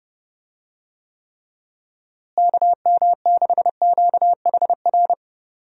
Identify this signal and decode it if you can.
Morse code, heard as a keyed tone, about 28 wpm, 700 Hz. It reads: KM6QHR
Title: USA Call Signs - 6 Characters - Repeated 2x - 30wpm